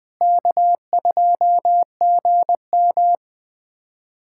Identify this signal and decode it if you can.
Morse code, heard as a keyed tone, about 20 wpm, 700 Hz. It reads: K2GM
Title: USA Callsign DE Callsign 20wpm